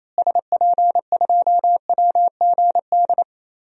Morse code, heard as a keyed tone, about 28 wpm, 700 Hz. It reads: SP2WGD
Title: DXCC-100 Call Signs - 6 Characters - Repeated 2x - 28wpm